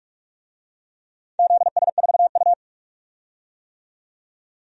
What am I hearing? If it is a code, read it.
Morse code, heard as a keyed tone, about 46 wpm, 700 Hz. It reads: ZS4U
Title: DXCC-100 Call Signs - 4 Characters - 45wpm